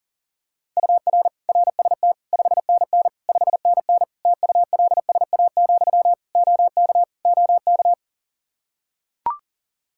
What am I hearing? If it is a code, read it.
Morse code, heard as a keyed tone, about 40 wpm, 700 Hz. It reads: UR RST 5NN 5NN TULSA, OK OK
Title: Practice QSO - Exchange 1 - Beginning - 40wpm